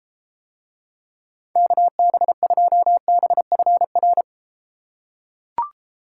Morse code, heard as a keyed tone, about 32 wpm, 700 Hz. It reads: KB2BFR